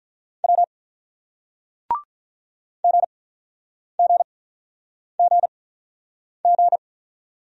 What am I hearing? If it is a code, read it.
Morse code, heard as a keyed tone, about 48 wpm, spaced as for 9 wpm, 700 Hz. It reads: W GGGG